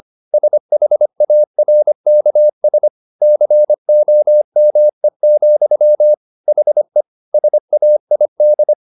Morse code, heard as a keyed tone, about 25 wpm, 600 Hz. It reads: SHARKS COME, HE SAID
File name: the old man and the sea part 1 25wpm